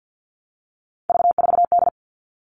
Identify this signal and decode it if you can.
Morse code, heard as a keyed tone, about 50 wpm, 700 Hz. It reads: V4D
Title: All Cut Numbers - 3-Digit 50wpm